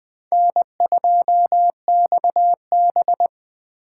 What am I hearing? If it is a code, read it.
Morse code, heard as a keyed tone, about 20 wpm, 700 Hz. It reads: N2XB
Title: USA Call Signs - 3x Character Spacing - 20wpm